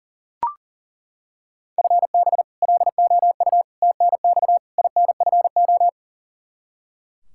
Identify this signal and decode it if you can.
Morse code, heard as a keyed tone, about 37 wpm, 700 Hz. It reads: FB LOU TNX INFO
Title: K1USN SST QSO Practice 40wpm